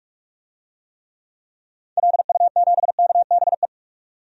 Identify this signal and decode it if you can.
Morse code, heard as a keyed tone, about 46 wpm, 700 Hz. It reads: LU7KBE
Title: DXCC-100 Call Signs - 6 Characters - 45wpm